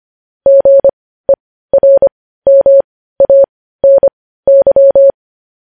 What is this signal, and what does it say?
Morse code, heard as a keyed tone, about 25 wpm, spaced as for 15 wpm, 550 Hz. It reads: GERMANY
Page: news headlines in morse code at 15 wpm